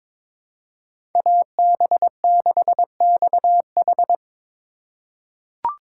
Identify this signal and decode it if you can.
Morse code, heard as a keyed tone, about 23 wpm, 700 Hz. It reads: AB6XH
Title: USA Call Signs - 5 Characters - 4x Character Spacing - 22wpm